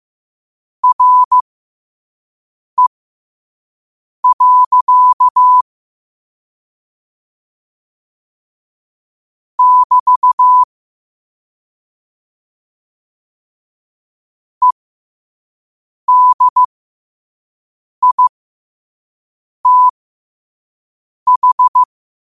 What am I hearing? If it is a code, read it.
Morse code, heard as a keyed tone, about 15 wpm, spaced as for 5 wpm, 1000 Hz. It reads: RE. <BT> EDITH
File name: qotdfiles / 5wpm